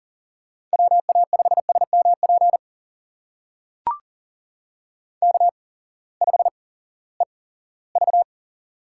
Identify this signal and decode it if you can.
Morse code, heard as a keyed tone, about 38 wpm, spaced as for 20 wpm, 700 Hz. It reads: WA5SMP K5EV
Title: USA Call Signs - 8x Character Spacing - 40wpm